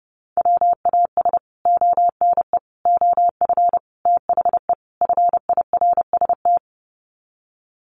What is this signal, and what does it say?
Morse code, heard as a keyed tone, about 30 wpm, 700 Hz. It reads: WAS ONE OF THE FIRST
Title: Sets of 5 Words 30wpm